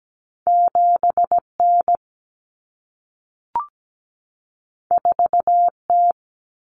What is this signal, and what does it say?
Morse code, heard as a keyed tone, about 17 wpm, 700 Hz. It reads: 7N 4T
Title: Common Cut Numbers - 2-Digit 17wpm